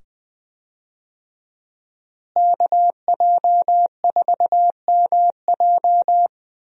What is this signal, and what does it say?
Morse code, heard as a keyed tone, about 20 wpm, 700 Hz. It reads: KJ4MJ
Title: USA Call Signs - 5 Characters - Repeated 2x - 20wpm